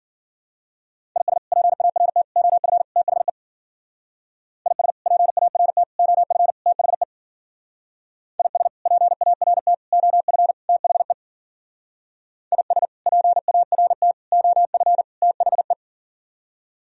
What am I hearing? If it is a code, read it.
Morse code, heard as a keyed tone, about 48 wpm, 700 Hz. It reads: IS PART OF THE IS PART OF THE IS PART OF THE IS PART OF THE
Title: Speed-Racing - Sets of 4 Words 40wpm